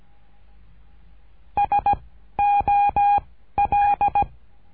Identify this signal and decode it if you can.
Morse code, heard as a keyed tone, about 16 wpm, 810 Hz. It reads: S O L
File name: cursos / CURSO DE TELEGRAFIA / Telegrafia - Trilha 13